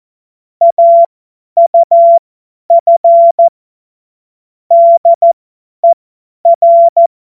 Morse code, heard as a keyed tone, about 14 wpm, spaced as for 10 wpm, 680 Hz. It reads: AUF DER
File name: Witz-23-September-2025-50-F70